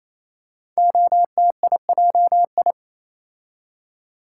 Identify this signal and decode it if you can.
Morse code, heard as a keyed tone, about 27 wpm, 700 Hz. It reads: OTIJI
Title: Code Groups - 5 Letters - 28wpm